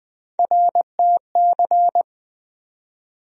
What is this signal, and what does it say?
Morse code, heard as a keyed tone, about 20 wpm, 700 Hz. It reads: RTC
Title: USA Call Sign Postfixes - 3 Letters 20wpm